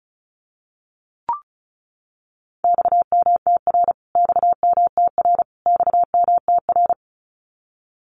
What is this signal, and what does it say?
Morse code, heard as a keyed tone, about 34 wpm, 700 Hz. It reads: XMTR XMTR XMTR
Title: Abbreviations Repeated 3x - 35wpm